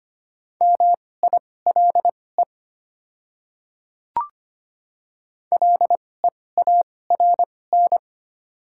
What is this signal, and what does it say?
Morse code, heard as a keyed tone, about 25 wpm, spaced as for 18 wpm, 700 Hz. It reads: MILE LEARN
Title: 2x Character Spacing - Top 300 Words - 25wpm